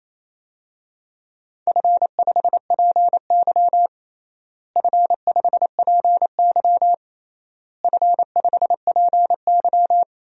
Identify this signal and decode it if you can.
Morse code, heard as a keyed tone, about 28 wpm, 700 Hz. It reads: F5PY F5PY F5PY
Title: DXCC-100 Call Signs - 4 Characters - Repeated 3x - 28wpm